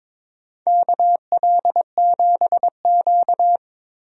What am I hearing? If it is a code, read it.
Morse code, heard as a keyed tone, about 22 wpm, 700 Hz. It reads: KL7Q
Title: DXCC-100 Call Signs - 4 Characters - Repeated 3x - 22wpm